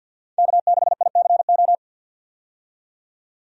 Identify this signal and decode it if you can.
Morse code, heard as a keyed tone, about 49 wpm, 700 Hz. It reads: K6ICO
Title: USA Call Signs - 5 Characters - 50wpm